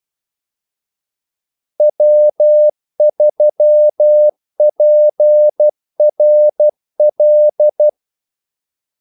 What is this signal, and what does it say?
Morse code, heard as a keyed tone, about 12 wpm, 600 Hz. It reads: W3PRL
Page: 2 ekstra ordmellemrum mellem kaldesignalerne